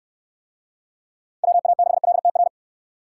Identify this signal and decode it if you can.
Morse code, heard as a keyed tone, about 68 wpm, 700 Hz. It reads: WA6LTR